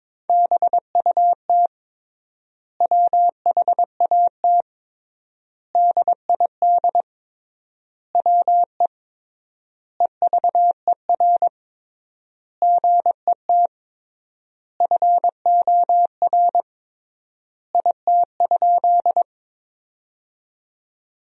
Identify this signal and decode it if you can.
Morse code, heard as a keyed tone, about 21 wpm, 700 Hz. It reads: BUT WHAT DID WE EVER GET FOR IT?
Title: 3x Word-Spacing - Sentences from Top 400 Words - 22wpm